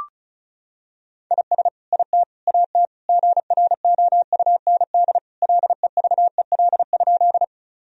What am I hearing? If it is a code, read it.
Morse code, heard as a keyed tone, about 34 wpm, 700 Hz. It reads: IS IT AT GROUND LEVEL?
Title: Sentences from Top 1000 Words - Review Part 4 - 35wpm